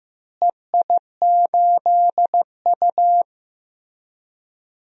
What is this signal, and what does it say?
Morse code, heard as a keyed tone, about 15 wpm, 700 Hz. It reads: EI8U
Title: DXCC-100 Call Signs - 4 Characters - Repeated 2x - 15wpm